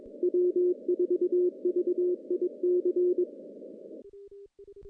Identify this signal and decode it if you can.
Morse code, heard as a keyed tone, about 22 wpm, 360 Hz. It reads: W4VIC
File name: contests / CW-SS-k5tr- -1500